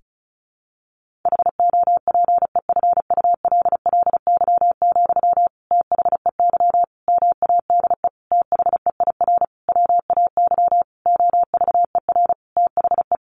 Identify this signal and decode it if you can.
Morse code, heard as a keyed tone, about 34 wpm, 700 Hz. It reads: HOPEFULLY, THEY MADE THEIR WAY OVER THE
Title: Sentences from Top 1500 Words 35wpm